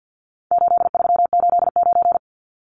Morse code, heard as a keyed tone, about 50 wpm, 700 Hz. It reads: 8389
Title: Numbers - 4-Digit 50wpm